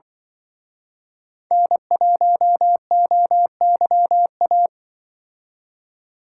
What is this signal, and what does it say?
Morse code, heard as a keyed tone, about 25 wpm, 700 Hz. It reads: N1OYA